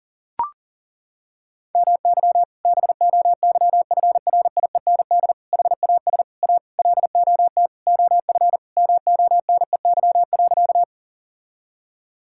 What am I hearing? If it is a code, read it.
Morse code, heard as a keyed tone, about 40 wpm, 700 Hz. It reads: MY BOYFRIEND HAS A LOT OF MONEY.
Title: Sentences from Top 1000 Words - Review Part 5 - 40wpm